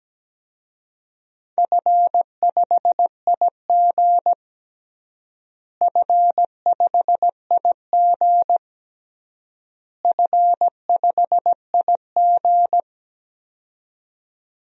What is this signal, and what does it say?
Morse code, heard as a keyed tone, about 17 wpm, 700 Hz. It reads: F5IG F5IG F5IG
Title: DXCC-100 Call Signs - 4 Characters - Repeated 3x - 17wpm